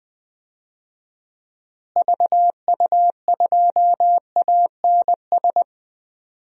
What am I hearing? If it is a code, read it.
Morse code, heard as a keyed tone, about 20 wpm, 700 Hz. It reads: VU2ANS
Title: DXCC-100 Call Signs - 6 Characters - Repeated 3x - 20wpm